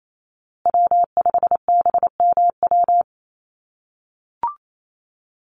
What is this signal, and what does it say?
Morse code, heard as a keyed tone, about 28 wpm, 700 Hz. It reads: W5BMW
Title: USA Call Signs Repeated 2x 28wpm